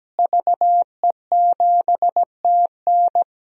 Morse code, heard as a keyed tone, about 17 wpm, 700 Hz. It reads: VE7TN
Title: DXCC-100 Call Signs - 5 Characters - Repeated 3x - 17wpm